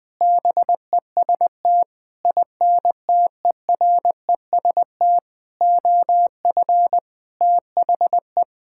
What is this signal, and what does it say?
Morse code, heard as a keyed tone, about 20 wpm, 700 Hz. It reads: BEST INTEREST OF THE
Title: Sentences with 7-Word Phrases - Spoken Beginning - 20wpm